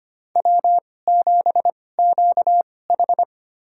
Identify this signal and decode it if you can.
Morse code, heard as a keyed tone, about 25 wpm, spaced as for 18 wpm, 700 Hz. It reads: W7QH
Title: USA Call Signs - 4 Characters - 2x Character Spacing - 25wpm